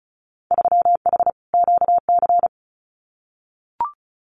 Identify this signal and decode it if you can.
Morse code, heard as a keyed tone, about 34 wpm, 690 Hz. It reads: 3H QC
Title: Winter Field Day Exchanges 35wpm